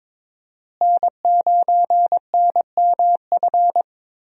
Morse code, heard as a keyed tone, about 22 wpm, 700 Hz. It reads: N9NMF
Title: USA Call Signs - 5 Characters - Repeated 2x - 22wpm